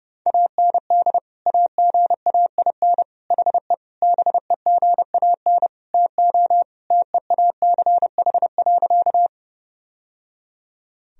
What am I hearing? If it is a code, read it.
Morse code, heard as a keyed tone, about 30 wpm, 700 Hz. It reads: AND AGAIN HE BEGAN TO TEACH.
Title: Sentences from Top 1000 Words - Review Part 2 - 30wpm